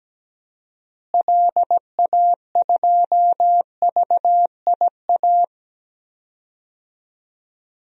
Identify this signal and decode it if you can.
Morse code, heard as a keyed tone, about 17 wpm, 700 Hz. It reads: LA2VIA
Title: DXCC-100 Call Signs - 6 Characters - Repeated 3x - 17wpm